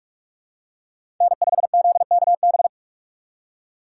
Morse code, heard as a keyed tone, about 44 wpm, 700 Hz. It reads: N5ZKB
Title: USA Call Signs - 5 Characters - 6x Character Spacing - 45wpm